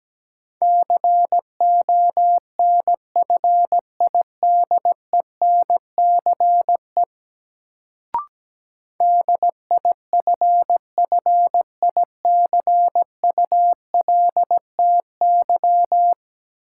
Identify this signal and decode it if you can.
Morse code, heard as a keyed tone, about 17 wpm, 700 Hz. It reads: CONFIDENCE DIFFICULTY
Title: Sentences from Top 1500 Words - New Word Drill Alt - 17wpm